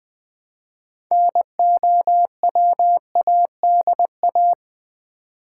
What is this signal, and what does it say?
Morse code, heard as a keyed tone, about 20 wpm, 690 Hz. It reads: NOWADA
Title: Build A Word - 3 Letters At A Time - 8-Letter Words - 20wpm